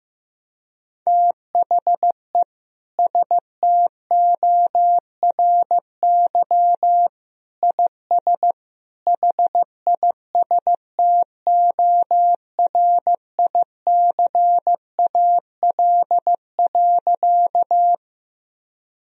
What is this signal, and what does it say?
Morse code, heard as a keyed tone, about 15 wpm, 700 Hz. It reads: THE STORY IS HISTORICAL.
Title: Sentences from Top 1000 Words - Review Part 4 - 15wpm